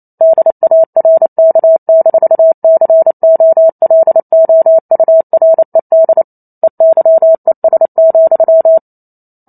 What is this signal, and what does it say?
Morse code, heard as a keyed tone, about 28 wpm, 650 Hz. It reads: DARK-COLOURED EYES,